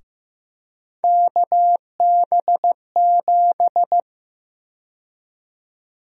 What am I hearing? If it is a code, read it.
Morse code, heard as a keyed tone, about 15 wpm, 700 Hz. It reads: KB7
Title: USA Call Sign Prefixes - 15wpm